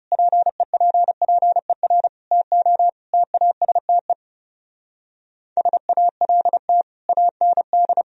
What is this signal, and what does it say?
Morse code, heard as a keyed tone, about 35 wpm, 700 Hz. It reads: PEPPER TO TASTE SALT AND
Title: Speed-Racing - Sets of 5 Words 30wpm